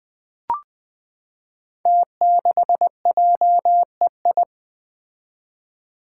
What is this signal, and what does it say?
Morse code, heard as a keyed tone, about 20 wpm, 700 Hz. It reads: T6JEI
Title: Five Challenging Letter-Number Combinations 20wpm